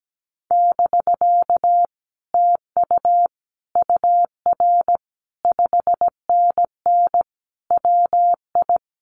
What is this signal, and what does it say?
Morse code, heard as a keyed tone, about 17 wpm, 700 Hz. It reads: <BK> TU UR 5NN WI
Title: Common QSO Phrases Encore 17wpm